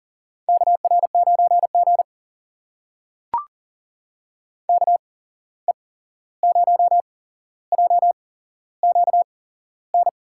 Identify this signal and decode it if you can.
Morse code, heard as a keyed tone, about 41 wpm, spaced as for 15 wpm, 700 Hz. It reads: KR9G KE0JQN
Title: USA Call Signs - 8x Character Spacing - 40wpm